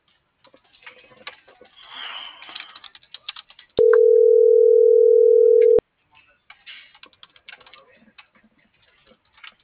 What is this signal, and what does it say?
Ringback tone (North America), 440 + 480 Hz.